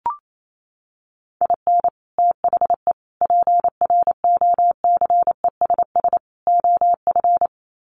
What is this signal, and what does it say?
Morse code, heard as a keyed tone, about 28 wpm, 700 Hz. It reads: IN THE PROCESS OF
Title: Sets of 4 Words 28wpm